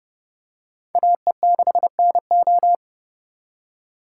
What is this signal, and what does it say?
Morse code, heard as a keyed tone, about 27 wpm, 700 Hz. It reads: AE6NO